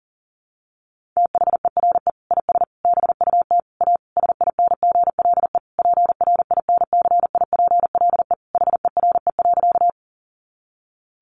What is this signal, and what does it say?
Morse code, heard as a keyed tone, about 40 wpm, 700 Hz. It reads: THERE IS BUT A SINGLE PRINCIPLE HERE.
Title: Sentences from Top 1400 Words 40wpm